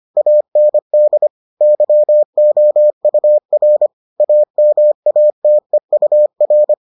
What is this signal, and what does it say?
Morse code, heard as a keyed tone, about 25 wpm, 600 Hz. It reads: AND YOUR AMATEUR